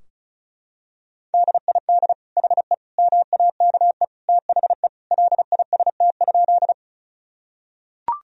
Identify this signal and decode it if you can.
Morse code, heard as a keyed tone, about 36 wpm, 700 Hz. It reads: DID HE MAKE THE LIST?
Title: Sentences from Top 1000 Words - Review Part 2 - 35wpm